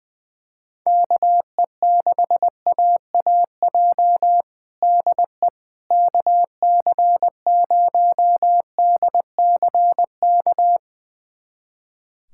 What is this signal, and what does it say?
Morse code, heard as a keyed tone, about 20 wpm, 700 Hz. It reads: KE6AAJ DE KC0DCK
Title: USA Callsign DE Callsign - 6 Characters - 20wpm